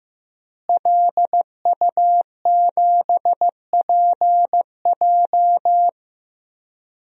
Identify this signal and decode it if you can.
Morse code, heard as a keyed tone, about 15 wpm, 700 Hz. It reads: LU7PJ
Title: DXCC-100 Call Signs - 5 Characters - Repeated 3x - 15wpm